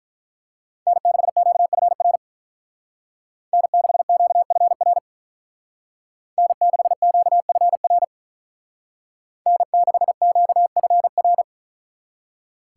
Morse code, heard as a keyed tone, about 43 wpm, 700 Hz. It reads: N6QFR N6QFR N6QFR N6QFR
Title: Speed Racing - USA 1x3 Call Signs 35wpm